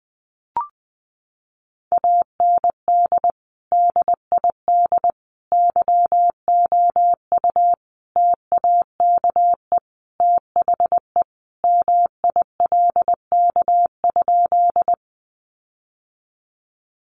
Morse code, heard as a keyed tone, about 20 wpm, 700 Hz. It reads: AND DID YOU TAKE THE MILK?
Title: Sentences from Top 1500 Words 20wpm